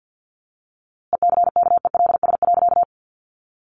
Morse code, heard as a keyed tone, about 50 wpm, 700 Hz. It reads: ECKELS.
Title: A Sound Of Thunder 50wpm